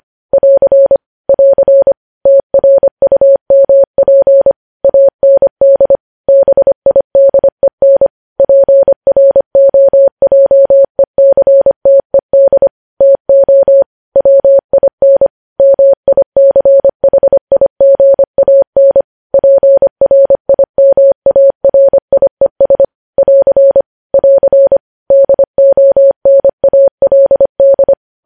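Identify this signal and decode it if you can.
Morse code, heard as a keyed tone, about 25 wpm, 550 Hz. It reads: <AR> <AR> TRUMP AND BIDEN PROJECTED TO WIN MICHIGAN PRIMARIES <AR> <AR> DONALD
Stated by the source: Morse code news headlines sent at 25 wpm.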